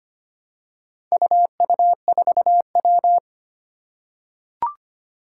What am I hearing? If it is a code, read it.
Morse code, heard as a keyed tone, about 25 wpm, 700 Hz. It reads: UU4W
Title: Four Challenging Letter-Number Combinations 25wpm